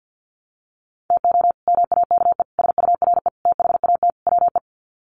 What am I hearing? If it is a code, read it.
Morse code, heard as a keyed tone, about 50 wpm, 700 Hz. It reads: TO MAKE SURE THAT WE
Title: Sets of 5 Words Encore 50wpm